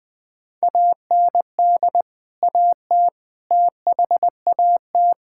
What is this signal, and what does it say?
Morse code, heard as a keyed tone, about 20 wpm, 700 Hz. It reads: AND AT THAT
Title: Sentences from Top 1000 Words - Review Part 5 - 20wpm